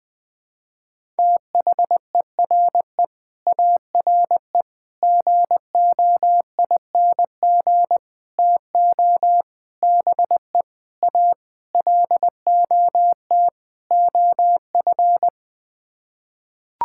Morse code, heard as a keyed tone, about 21 wpm, 700 Hz. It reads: THERE ARE GOING TO BE A LOT OF
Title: Sets of 8 Words - 20wpm